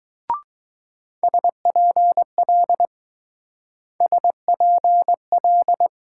Morse code, heard as a keyed tone, about 19 wpm, 700 Hz. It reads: SPL SPL